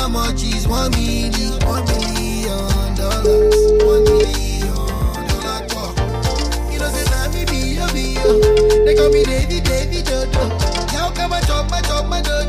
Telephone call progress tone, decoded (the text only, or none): ringback tone